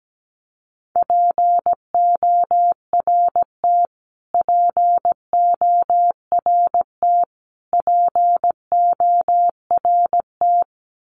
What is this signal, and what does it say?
Morse code, heard as a keyed tone, about 17 wpm, 700 Hz. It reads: PORT PORT PORT
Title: Top 200 Words Repeated 17wpm